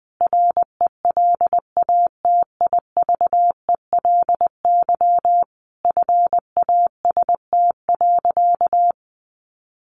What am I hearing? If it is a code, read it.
Morse code, heard as a keyed tone, about 20 wpm, 700 Hz. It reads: RELATIVELY FAST.
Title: Sentences from Top 1100 Words 20wpm